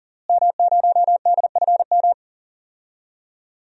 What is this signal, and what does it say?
Morse code, heard as a keyed tone, about 40 wpm, 700 Hz. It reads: M0DFM
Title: DXCC-100 Call Signs - 5 Characters - Repeated 3x - 40wpm